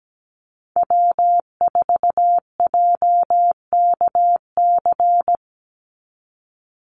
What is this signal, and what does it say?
Morse code, heard as a keyed tone, about 17 wpm, 700 Hz. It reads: W4JKC
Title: NA SOTA Chaser Callsigns 17wpm